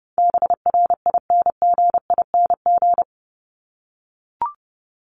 Morse code, heard as a keyed tone, about 30 wpm, 700 Hz. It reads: BRINGING
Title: ING Suffix - Top 100 Words with Base Words 30wpm